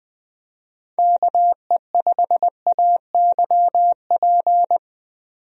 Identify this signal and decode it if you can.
Morse code, heard as a keyed tone, about 21 wpm, 700 Hz. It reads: KE5AYP
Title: USA Call Signs - 6 Characters - 20wpm